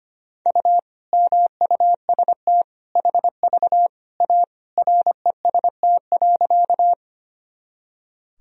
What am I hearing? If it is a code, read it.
Morse code, heard as a keyed tone, about 25 wpm, 700 Hz. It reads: U MUST HV A REST.
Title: Sentences from Top 500 Words and Abbreviations 25wpm